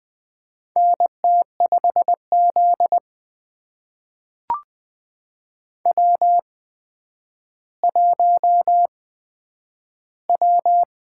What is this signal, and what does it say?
Morse code, heard as a keyed tone, about 21 wpm, spaced as for 10 wpm, 700 Hz. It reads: NT5Z W1W
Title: USA Call Signs - 4 Characters - 8x Character Spacing - 20wpm